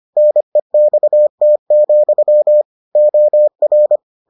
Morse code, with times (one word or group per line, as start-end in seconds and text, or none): 0.2-2.6 NEXT,
3.0-4.0 OR